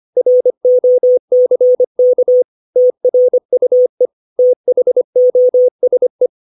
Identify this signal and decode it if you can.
Morse code, heard as a keyed tone, about 25 wpm, 500 Hz. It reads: ROCK TRUE THOSE
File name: 632 words / 632 Words 25WPM